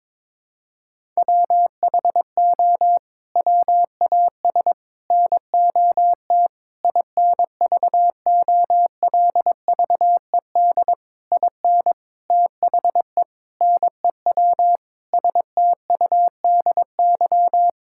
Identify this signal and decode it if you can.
Morse code, heard as a keyed tone, about 22 wpm, 700 Hz. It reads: WHO WAS NOT INVOLVED IN THE NEW STUDY
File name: practice / 22wpm / Sets of 8 Words - 22wpm